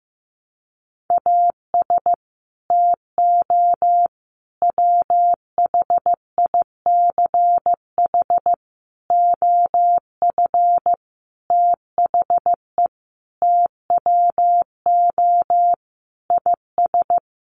Morse code, heard as a keyed tone, about 15 wpm, 700 Hz. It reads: AS TO WHICH OF THE TWO IS
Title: Sentences from Top 1200 Words 15wpm